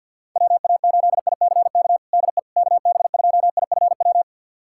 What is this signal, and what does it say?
Morse code, heard as a keyed tone, about 51 wpm, 700 Hz. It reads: WA8ICK DE KB2IFW
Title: USA Callsign DE Callsign 50wpm